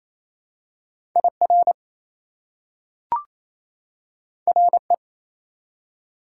Morse code, heard as a keyed tone, about 28 wpm, 700 Hz. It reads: IR RE
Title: Prefixes and Suffixes 28wpm